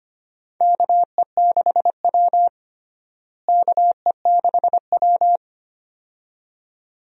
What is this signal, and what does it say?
Morse code, heard as a keyed tone, about 25 wpm, 700 Hz. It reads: KE6W KE6W
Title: USA Call Signs - 4 Characters - Repeated 2x - 25wpm